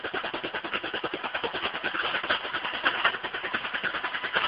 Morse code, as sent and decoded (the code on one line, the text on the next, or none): none